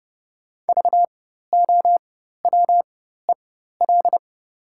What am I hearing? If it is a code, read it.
Morse code, heard as a keyed tone, about 29 wpm, spaced as for 14 wpm, 690 Hz. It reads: VOWEL
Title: 4x Character Spacing - Top 500 Words - 30wpm